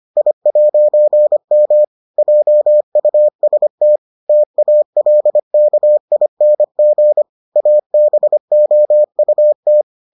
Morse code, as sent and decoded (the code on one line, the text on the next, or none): .. .----. -- / .--- ..- ... - / - .- .-.. -.- .. -. --. / .- -... --- ..- -
I'M JUST TALKING ABOUT